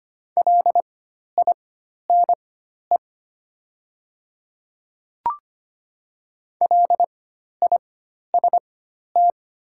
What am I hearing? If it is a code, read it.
Morse code, heard as a keyed tone, about 25 wpm, spaced as for 12 wpm, 700 Hz. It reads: LINE LIST
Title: 4x Character Spacing - Top 500 Words - 25wpm